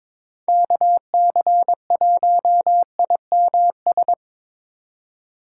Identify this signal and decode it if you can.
Morse code, heard as a keyed tone, about 25 wpm, 700 Hz. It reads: KC1IMS
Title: Speed Racing - USA 2x3 Call Signs 22wpm